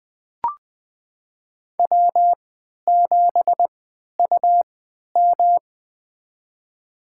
Morse code, heard as a keyed tone, about 20 wpm, spaced as for 11 wpm, 700 Hz. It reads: W7UM
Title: USA Call Signs - 4 Characters - 3x Character Spacing - 20wpm